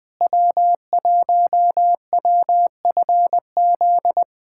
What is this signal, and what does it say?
Morse code, heard as a keyed tone, about 20 wpm, 700 Hz. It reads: W1WFZ
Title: USA Callsign DE Callsign - 5 Characters - 20wpm